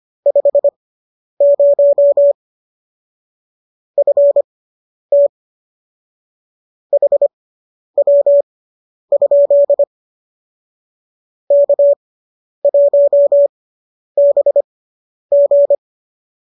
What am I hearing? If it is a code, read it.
Morse code, heard as a keyed tone, about 25 wpm, spaced as for 10 wpm, 590 Hz. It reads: 50 FT HW? K1BG